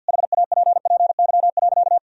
Morse code, heard as a keyed tone, about 50 wpm, 700 Hz. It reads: HAPPY.
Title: Sentences from Top 1000 Words - Review Part 2 - 50wpm